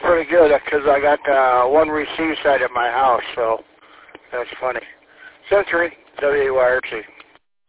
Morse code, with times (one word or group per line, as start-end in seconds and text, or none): none